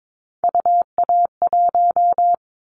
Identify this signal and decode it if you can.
Morse code, heard as a keyed tone, about 22 wpm, 700 Hz. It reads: UA1
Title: Callsign Prefixes for Top 50 DXCC - 22wpm